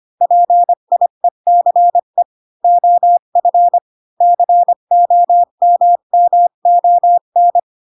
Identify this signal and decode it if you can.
Morse code, heard as a keyed tone, about 25 wpm, 700 Hz. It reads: PIECE OF COMMON
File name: Pinocchio / Pinocchio ch1 20 25 ik1hge